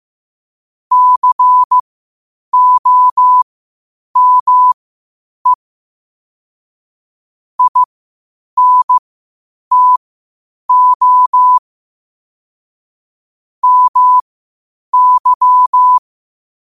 Morse code, heard as a keyed tone, about 15 wpm, spaced as for 8 wpm, 1000 Hz. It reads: COME INTO MY
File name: qotdfiles / 8wpm